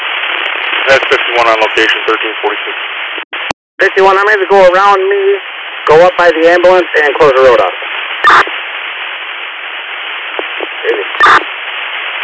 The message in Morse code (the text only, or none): none